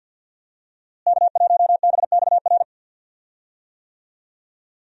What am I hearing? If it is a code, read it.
Morse code, heard as a keyed tone, about 52 wpm, 700 Hz. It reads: K1BXR
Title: USA Call Signs - 5 Characters - Repeated 3x - 50wpm